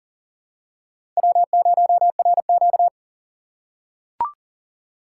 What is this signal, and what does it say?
Morse code, heard as a keyed tone, about 40 wpm, 700 Hz. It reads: W0RQ
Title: USA Call Signs - 4 Characters - 4x Character Spacing - 40wpm